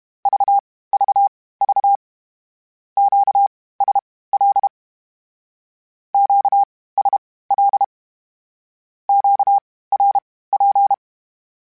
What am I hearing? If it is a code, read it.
Morse code, heard as a keyed tone, about 32 wpm, spaced as for 18 wpm, 800 Hz. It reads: VVV QSL QSL QRP
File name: (spaced) / Q-Calls Spaced-mp3s / Q Calls Twice sp @30